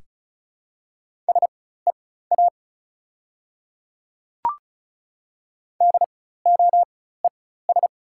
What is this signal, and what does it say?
Morse code, heard as a keyed tone, about 36 wpm, spaced as for 16 wpm, 690 Hz. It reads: SEA DOES
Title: 4x Character Spacing - Top 300 Words - 35wpm